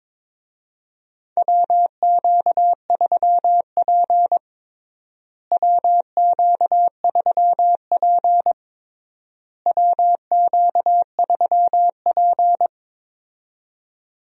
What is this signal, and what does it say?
Morse code, heard as a keyed tone, about 22 wpm, 700 Hz. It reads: WQ3P WQ3P WQ3P
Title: USA Call Signs - 4 Characters - Repeated 3x - 22wpm